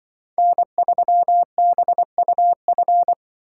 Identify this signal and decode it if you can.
Morse code, heard as a keyed tone, about 24 wpm, 700 Hz. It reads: N3BUF
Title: Speed Racing - USA Call Signs 20wpm